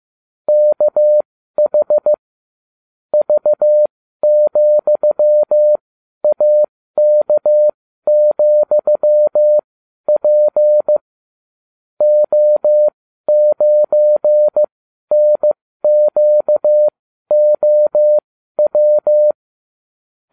Morse code, heard as a keyed tone, about 15 wpm, spaced as for 12 wpm, 600 Hz. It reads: KH V,AK,P O9NQOW